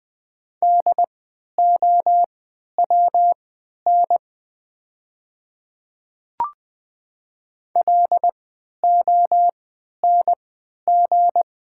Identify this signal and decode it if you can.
Morse code, heard as a keyed tone, about 20 wpm, spaced as for 11 wpm, 700 Hz. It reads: DOWN LONG
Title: 3x Character Spacing - Top 100 Words - 20wpm